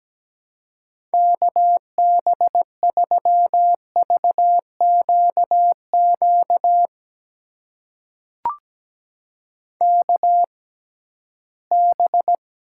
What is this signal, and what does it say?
Morse code, heard as a keyed tone, about 17 wpm, spaced as for 13 wpm, 700 Hz. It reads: KB3VQQ KB
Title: USA Call Signs - 6x Character Spacing - 17wpm